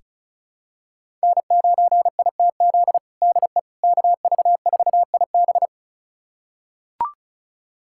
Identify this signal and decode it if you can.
Morse code, heard as a keyed tone, about 36 wpm, 690 Hz. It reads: N9ITZ DE KV4IB
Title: USA Callsign DE Callsign - 5 Characters - 35wpm